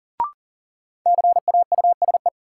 Morse code, heard as a keyed tone, about 40 wpm, 700 Hz. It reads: CAUSE
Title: Sentences from Top 200 Words - New Word Drill - 40wpm